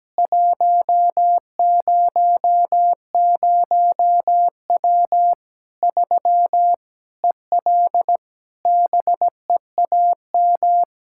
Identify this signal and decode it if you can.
Morse code, heard as a keyed tone, about 17 wpm, 700 Hz. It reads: 100W 3 EL BEAM
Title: Practice QSO - Exchange 2 - Info - 17wpm